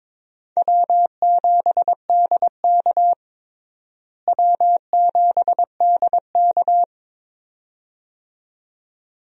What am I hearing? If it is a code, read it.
Morse code, heard as a keyed tone, about 22 wpm, 700 Hz. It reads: W7DK W7DK
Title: USA Call Signs - 4 Characters - Repeated 2x - 22wpm